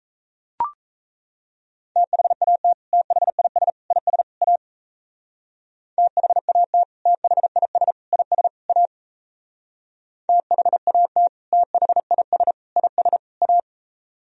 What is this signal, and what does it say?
Morse code, heard as a keyed tone, about 35 wpm, 700 Hz. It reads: THAT THIS IS A THAT THIS IS A THAT THIS IS A
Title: Speed-Racing - Sets of 4 Words 28wpm